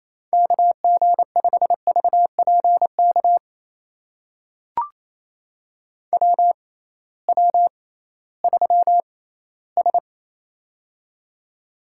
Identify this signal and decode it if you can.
Morse code, heard as a keyed tone, about 28 wpm, spaced as for 16 wpm, 700 Hz. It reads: KG5VPK WW3S
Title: USA Call Signs - 6x Character Spacing - 28wpm